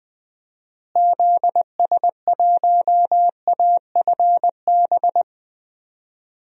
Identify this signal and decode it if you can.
Morse code, heard as a keyed tone, about 21 wpm, 700 Hz. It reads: ZS1AFB
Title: DXCC-100 Call Signs - 6 Characters - Repeated 3x - 20wpm